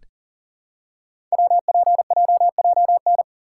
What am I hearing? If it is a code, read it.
Morse code, heard as a keyed tone, about 39 wpm, 700 Hz. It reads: WPJJN
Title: Code Groups - 5 Letters - 40wpm